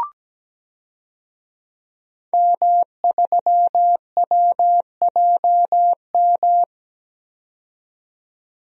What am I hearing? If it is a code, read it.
Morse code, heard as a keyed tone, about 17 wpm, 700 Hz. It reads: M3WJM
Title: DXCC-100 Call Signs - 5 Characters - 17wpm